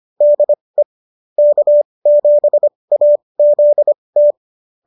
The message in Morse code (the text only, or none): DE K7AZT